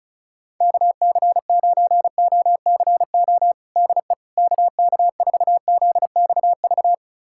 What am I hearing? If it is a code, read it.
Morse code, heard as a keyed tone, about 36 wpm, 700 Hz. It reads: KC9OCO DE KK4ZXV
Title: USA Callsign DE Callsign 35wpm